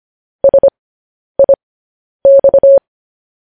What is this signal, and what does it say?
Morse code, heard as a keyed tone, about 25 wpm, spaced as for 10 wpm, 550 Hz. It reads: SIX
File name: podcast / News headlines-1x-25x10